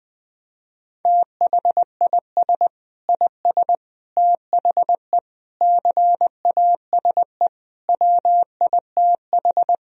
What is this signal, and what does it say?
Morse code, heard as a keyed tone, about 21 wpm, 700 Hz. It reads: THIS IS THE CASE WITH
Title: Sentences from Top 1000 Words - Review Part 3 - 20wpm